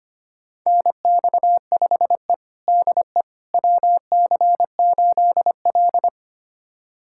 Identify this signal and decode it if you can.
Morse code, heard as a keyed tone, about 25 wpm, 700 Hz. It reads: NX5E DE WC8L
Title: USA Callsign DE Callsign - 4 Characters - 25wpm